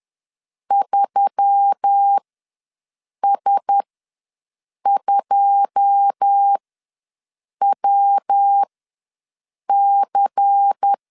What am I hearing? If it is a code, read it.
Morse code, heard as a keyed tone, about 10 wpm, spaced as for 6 wpm, 800 Hz. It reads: 3S2WC
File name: LEZIONI / 10 SIGNS / 10C DOT ws 10wpm 4213